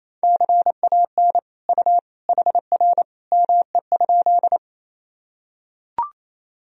Morse code, heard as a keyed tone, about 28 wpm, 700 Hz. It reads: CAN U HR ME?
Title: Sentences from Top 300 Words and Abbreviations 28wpm